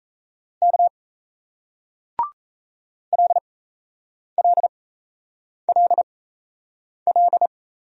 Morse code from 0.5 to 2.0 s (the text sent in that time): K